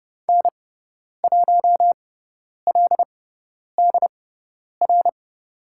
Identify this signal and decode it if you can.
Morse code, heard as a keyed tone, about 30 wpm, spaced as for 10 wpm, 700 Hz. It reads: N1LDR
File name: Lesson 17g - Letter u - Call Signs 30wpm